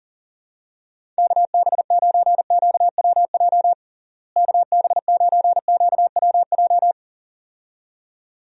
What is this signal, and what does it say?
Morse code, heard as a keyed tone, about 40 wpm, 700 Hz. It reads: KB9QWJ KB9QWJ
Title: USA Call Signs - 6 Characters - Repeated 2x - 40wpm